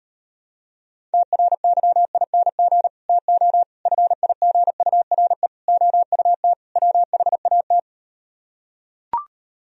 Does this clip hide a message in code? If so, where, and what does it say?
Yes: Morse code from 1.1 to 7.8 s, heard as a keyed tone, about 38 wpm, 700 Hz. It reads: TRYING TO FIGURE OUT WHAT